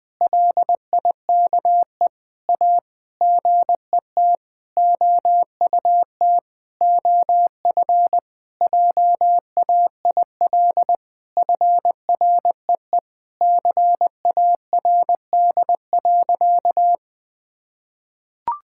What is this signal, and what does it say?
Morse code, heard as a keyed tone, about 20 wpm, 700 Hz. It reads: LIKE A GET OUT OF JAIL FREE CARD.
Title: Sentences with 7-Word Phrases - 20wpm